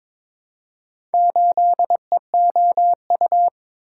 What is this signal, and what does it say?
Morse code, heard as a keyed tone, about 22 wpm, 700 Hz. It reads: 8EOU
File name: practice / 22wpm / Four Letter-Number Combinations - 22wpm